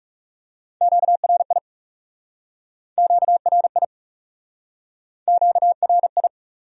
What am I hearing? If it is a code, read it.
Morse code, heard as a keyed tone, about 41 wpm, 700 Hz. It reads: QRI QRI QRI